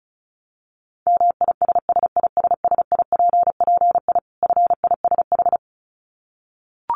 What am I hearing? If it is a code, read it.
Morse code, heard as a keyed tone, about 34 wpm, 700 Hz. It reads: MISSISSIPPI FISH
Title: Dit Challenge 35wpm